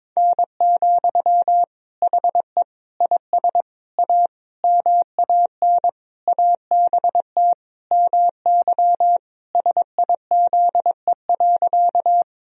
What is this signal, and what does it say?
Morse code, heard as a keyed tone, about 22 wpm, 700 Hz. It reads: N, HE IS A MAN ABT MY SIZE.
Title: Sentences from Top 400 Words and Abbreviations 22wpm